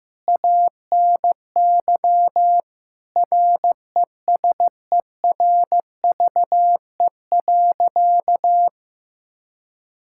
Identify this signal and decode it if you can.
Morse code, heard as a keyed tone, about 15 wpm, 700 Hz. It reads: ANY RESERVE.
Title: Sentences from Top 1000 Words - Review Part 6 - 15wpm